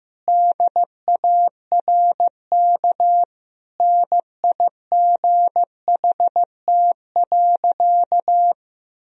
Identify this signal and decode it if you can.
Morse code, heard as a keyed tone, about 15 wpm, 700 Hz. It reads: DARK NIGHT.
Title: Sentences from Top 500 Words and Abbreviations 15wpm